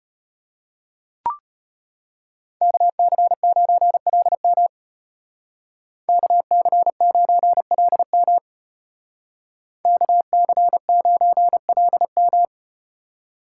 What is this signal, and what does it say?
Morse code, heard as a keyed tone, about 34 wpm, 700 Hz. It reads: KC9LM KC9LM KC9LM